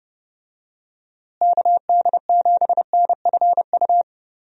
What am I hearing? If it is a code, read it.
Morse code, heard as a keyed tone, about 32 wpm, 700 Hz. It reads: KD7NFU
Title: USA Call Signs - 6 Characters - Repeated 2x - 30wpm